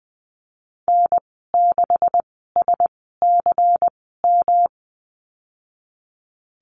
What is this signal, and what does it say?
Morse code, heard as a keyed tone, about 19 wpm, spaced as for 14 wpm, 700 Hz. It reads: N6SCM
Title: 167 Lesson 27g - Number 7 - Call Signs 20wpm